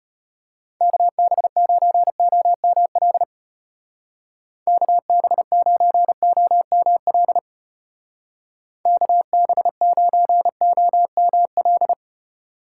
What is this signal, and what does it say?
Morse code, heard as a keyed tone, about 30 wpm, 700 Hz. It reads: KB9OML KB9OML KB9OML
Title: Speed Racing - USA 2x3 Call Signs 25wpm